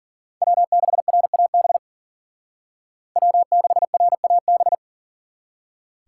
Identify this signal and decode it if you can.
Morse code, heard as a keyed tone, about 53 wpm, 700 Hz. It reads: W6RAB W6RAB